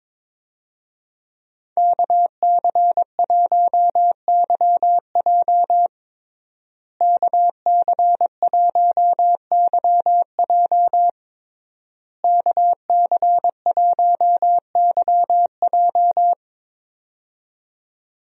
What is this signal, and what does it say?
Morse code, heard as a keyed tone, about 22 wpm, 700 Hz. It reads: KC1YJ KC1YJ KC1YJ
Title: USA Call Signs - 5 Characters - Repeated 3x - 22wpm